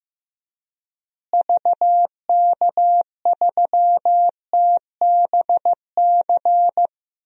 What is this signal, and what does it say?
Morse code, heard as a keyed tone, about 15 wpm, 700 Hz. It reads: VK3TBC
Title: DXCC-100 Call Signs - 6 Characters - Repeated 3x - 15wpm